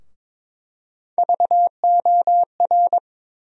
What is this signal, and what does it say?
Morse code, heard as a keyed tone, about 22 wpm, 700 Hz. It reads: VOR
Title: USA Call Sign Postfixes - 3 Letters 22wpm